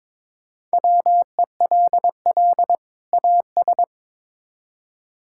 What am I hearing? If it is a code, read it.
Morse code, heard as a keyed tone, about 22 wpm, 700 Hz. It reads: WELL AS
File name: Build A Phrase - 2-Word Phrases - 22wpm